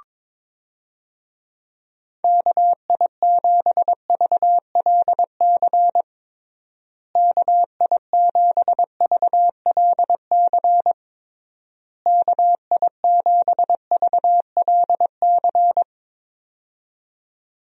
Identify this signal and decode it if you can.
Morse code, heard as a keyed tone, about 22 wpm, 700 Hz. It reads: KI7VLC KI7VLC KI7VLC
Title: USA Call Signs - 6 Characters - Repeated 3x - 22wpm